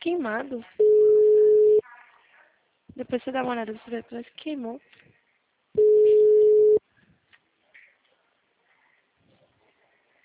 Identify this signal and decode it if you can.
Ringback tone (Europe, 425 Hz), with speech.
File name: out- -2012- - - .282